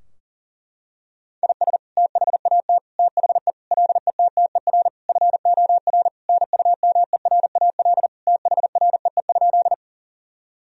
Morse code, heard as a keyed tone, about 37 wpm, 700 Hz. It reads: IS THAT THE LETTER FOR NUMERAL THREE?
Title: Sentences from Top 1000 Words - Review Part 2 - 40wpm